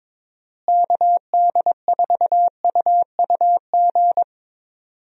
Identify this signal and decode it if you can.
Morse code, heard as a keyed tone, about 22 wpm, 700 Hz. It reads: KD4UUG
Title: USA Call Signs - 6 Characters - Repeated 2x - 22wpm